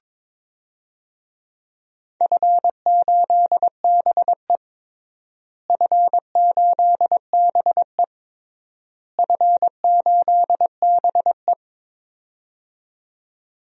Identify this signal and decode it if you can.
Morse code, heard as a keyed tone, about 22 wpm, 700 Hz. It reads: F8BE F8BE F8BE
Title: DXCC-100 Call Signs - 4 Characters - Repeated 3x - 22wpm